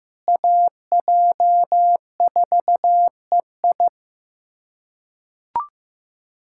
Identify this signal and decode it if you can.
Morse code, heard as a keyed tone, about 15 wpm, 700 Hz. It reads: AJ4EI
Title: USA Callsign DE Callsign - 5 Characters - 15wpm